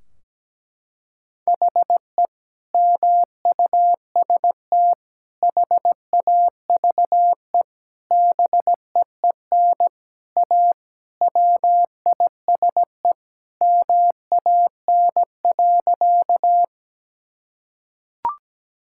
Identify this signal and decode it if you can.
Morse code, heard as a keyed tone, about 17 wpm, 700 Hz. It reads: HE MUST HAVE BEEN A WISE MAN.
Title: Sentences from Top 1600 Words 17wpm